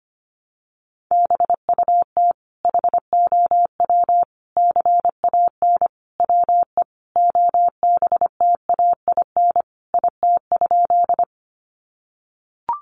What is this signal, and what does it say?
Morse code, heard as a keyed tone, about 25 wpm, 700 Hz. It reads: BUT HOW CAN WE OBTAIN IT?
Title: Sentences from Top 1300 Words 25wpm